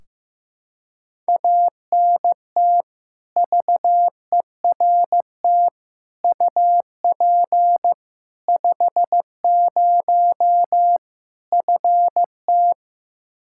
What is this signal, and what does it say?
Morse code, heard as a keyed tone, about 15 wpm, 700 Hz. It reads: ANT VERT UP 50 FT
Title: Practice QSO - Exchange 2 - Info - 15wpm